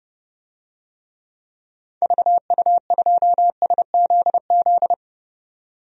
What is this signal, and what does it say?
Morse code, heard as a keyed tone, about 30 wpm, 700 Hz. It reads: VU2SZZ
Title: DXCC-100 Call Signs - 6 Characters - 30wpm